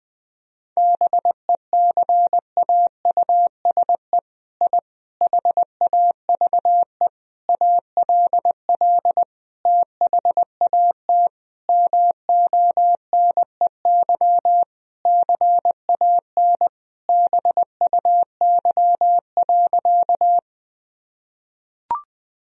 Morse code, heard as a keyed tone, about 21 wpm, 700 Hz. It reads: BECAUSE I HAVE ALL THAT MONEY CAN BUY.
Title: Sentences from Top 1000 Words - Review Part 4 - 20wpm